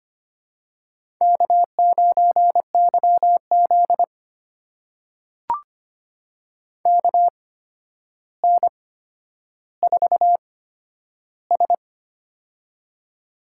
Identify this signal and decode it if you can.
Morse code, heard as a keyed tone, about 25 wpm, spaced as for 11 wpm, 700 Hz. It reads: K9YZ KN4S
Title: USA Call Signs - 4 Characters - 8x Character Spacing - 25wpm